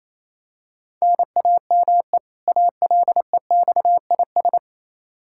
Morse code, heard as a keyed tone, about 28 wpm, 700 Hz. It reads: NAME ALEXIS
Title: Practice QSO - Exchange 1 - Beginning - 28wpm